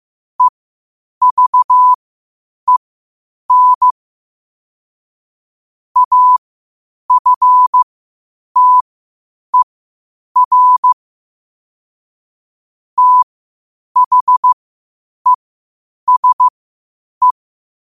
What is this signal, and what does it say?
Morse code, heard as a keyed tone, about 15 wpm, spaced as for 8 wpm, 1000 Hz. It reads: EVEN AFTER THESE
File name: qotdfiles / 8wpm